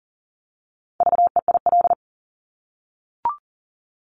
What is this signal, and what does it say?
Morse code, heard as a keyed tone, about 40 wpm, 700 Hz. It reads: VEIL
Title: Multi-letter Phonemes Repeated 3x 40wpm